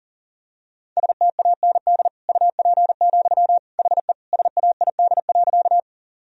Morse code, heard as a keyed tone, about 37 wpm, 700 Hz. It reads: STAND UP, HE SAID.
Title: Sentences from Top 1000 Words - Review Part 1 - 40wpm